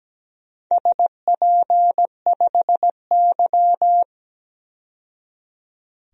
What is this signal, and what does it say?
Morse code, heard as a keyed tone, about 17 wpm, 700 Hz. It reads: SP5Y
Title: DXCC-100 Call Signs - 4 Characters - 17wpm